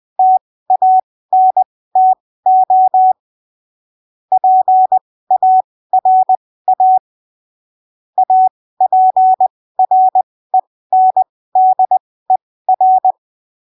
Morse code, heard as a keyed tone, about 20 wpm, spaced as for 15 wpm, 750 Hz. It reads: TANTO PARA APRENDER